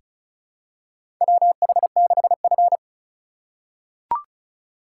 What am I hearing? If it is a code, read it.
Morse code, heard as a keyed tone, about 34 wpm, 700 Hz. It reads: WH6F
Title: USA Call Signs - 4 Characters - 8x Character Spacing - 35wpm